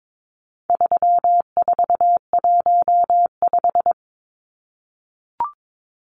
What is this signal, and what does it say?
Morse code, heard as a keyed tone, about 22 wpm, 700 Hz. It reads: 3415
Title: Numbers - 4-Digit 22wpm